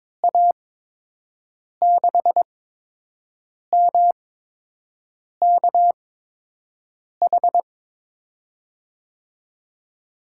Morse code, heard as a keyed tone, about 21 wpm, spaced as for 6 wpm, 700 Hz. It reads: A6MKH
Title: USA Call Signs - 6 Characters - 8x Character Spacing - 22wpm